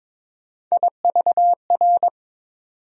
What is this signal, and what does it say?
Morse code, heard as a keyed tone, about 22 wpm, 700 Hz. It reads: IVR
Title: USA Call Sign Postfixes - 3 Letters 22wpm